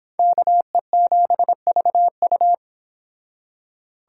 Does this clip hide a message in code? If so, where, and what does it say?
Yes: Morse code from 0.2 to 2.5 s, heard as a keyed tone, about 31 wpm, 700 Hz. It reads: KE7VU